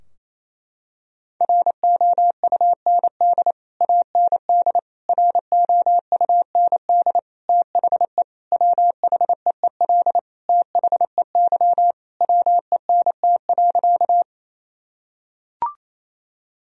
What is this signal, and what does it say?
Morse code, heard as a keyed tone, about 27 wpm, 700 Hz. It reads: ROUND AND ROUND THE WHEEL THEY WENT.
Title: Sentences from Top 1000 Words - Review Part 3 - 28wpm